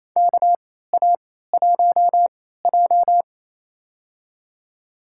Morse code, heard as a keyed tone, about 27 wpm, spaced as for 16 wpm, 700 Hz. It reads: KA1J
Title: USA Call Signs - 4 Characters - 3x Character Spacing - 28wpm